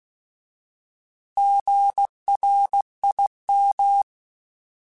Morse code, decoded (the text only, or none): GRIM